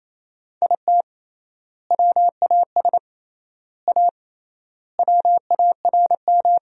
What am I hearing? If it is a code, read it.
Morse code, heard as a keyed tone, about 28 wpm, 700 Hz. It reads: IT WAS A WARM
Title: 3x Word-Spacing - Sentences from Top 500 Words - 28wpm